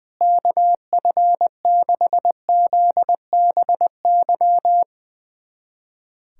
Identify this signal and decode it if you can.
Morse code, heard as a keyed tone, about 20 wpm, 700 Hz. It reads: KF6ZBY
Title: USA Callsign DE Callsign - 6 Characters - 20wpm